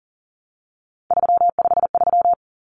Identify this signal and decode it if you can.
Morse code, heard as a keyed tone, about 40 wpm, 700 Hz. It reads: 353
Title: Numbers - 3-Digit 40wpm